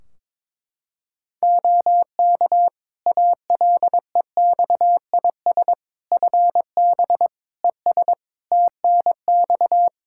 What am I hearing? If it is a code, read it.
Morse code, heard as a keyed tone, about 22 wpm, 700 Hz. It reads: OK ALEXIS FB ES TNX
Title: Practice QSO - Exchange 2 - Info - 22wpm